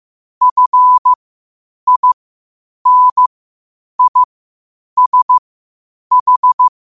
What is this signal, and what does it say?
Morse code, heard as a keyed tone, about 15 wpm, spaced as for 8 wpm, 1000 Hz. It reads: FINISH
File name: qotdfiles / 8wpm